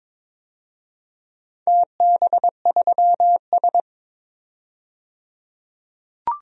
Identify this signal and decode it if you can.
Morse code, heard as a keyed tone, about 22 wpm, 700 Hz. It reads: TB3S
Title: DXCC-100 Call Signs - 4 Characters - 22wpm